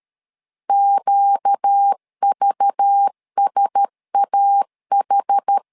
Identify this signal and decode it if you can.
Morse code, heard as a keyed tone, about 13 wpm, 800 Hz. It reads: QVSAH
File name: LEZIONI / 5 LBVC / 5 LBV 12wpm 1855